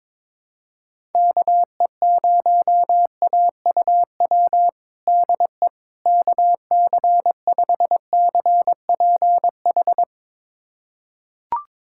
Morse code, heard as a keyed tone, about 22 wpm, 700 Hz. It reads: KE0AUW DE KC5CPH
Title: USA Callsign DE Callsign - 6 Characters - 22wpm